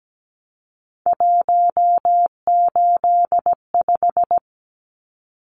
Morse code, heard as a keyed tone, about 17 wpm, 700 Hz. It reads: 185
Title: Numbers - 3-Digit 17wpm